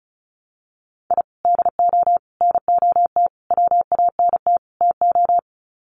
Morse code, heard as a keyed tone, about 34 wpm, 700 Hz. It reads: I DO NOT WANT TO
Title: Sets of 5 Words Encore2 35wpm